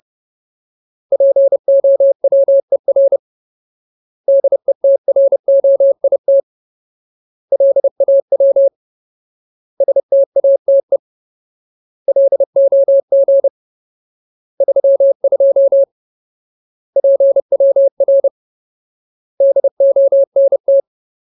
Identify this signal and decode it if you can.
Morse code, heard as a keyed tone, about 30 wpm, 550 Hz. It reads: POWER DETROIT LAW STATE LOG 32 PWR DONT
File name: Ultimate Word List / 30WPM Ultimate